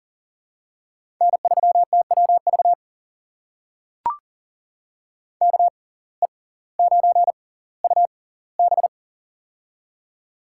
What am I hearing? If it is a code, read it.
Morse code, heard as a keyed tone, about 40 wpm, spaced as for 20 wpm, 700 Hz. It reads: N3TWV KE9UB
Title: USA Call Signs - 5 Characters - 6x Character Spacing - 40wpm